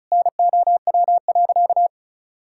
Morse code, heard as a keyed tone, about 36 wpm, 700 Hz. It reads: NOW.
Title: Sentences from Top 1000 Words - Review Part 6 - 35wpm